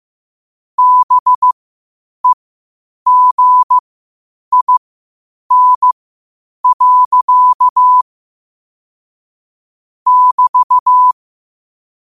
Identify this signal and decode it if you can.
Morse code, heard as a keyed tone, about 15 wpm, spaced as for 8 wpm, 1000 Hz. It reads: BEGIN. <BT>
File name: qotdfiles / 8wpm